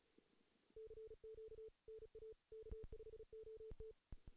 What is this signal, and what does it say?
Morse code, heard as a keyed tone, about 37 wpm, 440 Hz. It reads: CQ NA K5OT
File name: 2016 JAN CW NAQP / 2016 Jan naqp k5ot - 0-3